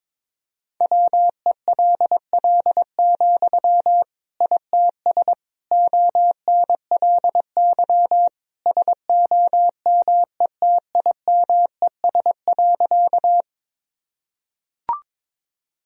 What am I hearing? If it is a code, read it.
Morse code, heard as a keyed tone, about 22 wpm, 700 Hz. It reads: WELL, ITS ONLY SOMETIMES.
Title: Sentences from Top 1000 Words - Review Part 3 - 22wpm